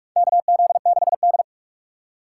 Morse code, heard as a keyed tone, about 46 wpm, 700 Hz. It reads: KZ6D
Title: USA Call Signs - 4 Characters - 2x Character Spacing - 45wpm